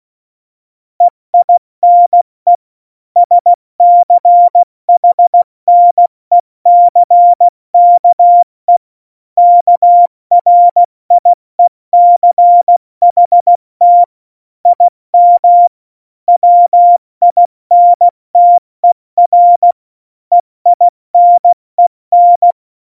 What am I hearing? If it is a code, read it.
Morse code, heard as a keyed tone, about 16 wpm, 700 Hz. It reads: EINE SCHNECKE KRIECHT IM WINTER EINEN
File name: Morsekurs / Witze / Witz 14. Januar 2020, 70 F80